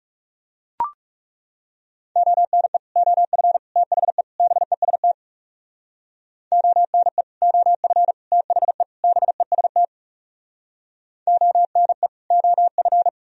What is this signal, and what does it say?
Morse code, heard as a keyed tone, about 40 wpm, 700 Hz. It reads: ONE OF THE BEST ONE OF THE BEST ONE OF